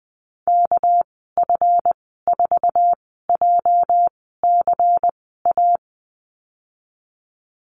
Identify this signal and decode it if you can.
Morse code, heard as a keyed tone, about 19 wpm, spaced as for 14 wpm, 700 Hz. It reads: KF4JCA
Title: 231 Lesson 37g - Letter x - Call Signs 20wpm